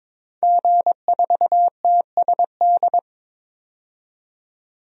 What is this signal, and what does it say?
Morse code, heard as a keyed tone, about 21 wpm, 700 Hz. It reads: G4TSD
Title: DXCC-100 Call Signs - 5 Characters - 22wpm